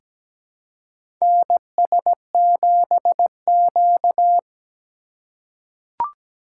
Morse code, heard as a keyed tone, about 17 wpm, 700 Hz. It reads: NS7Q
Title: USA Call Signs - 4 Characters - 4x Character Spacing - 17wpm